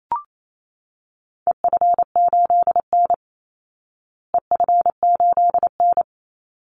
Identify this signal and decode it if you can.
Morse code, heard as a keyed tone, about 28 wpm, 700 Hz. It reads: EF8N EF8N
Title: Call Signs Repeated 3x - 28wpm